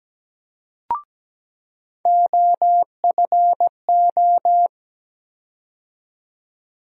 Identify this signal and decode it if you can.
Morse code, heard as a keyed tone, about 17 wpm, 700 Hz. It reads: OFO
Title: USA Call Sign Postfixes - 3 Letters 17wpm